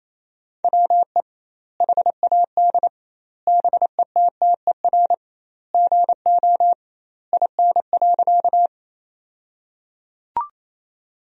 Morse code, heard as a keyed tone, about 27 wpm, 700 Hz. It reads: WE HAD BETTER GO IN.
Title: 2x Word-Spacing - Sentences from Top 400 Words - 28wpm